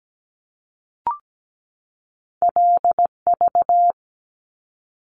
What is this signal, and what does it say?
Morse code, heard as a keyed tone, about 17 wpm, 700 Hz. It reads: LV
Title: Top Two Letter Combinations in Words 17wpm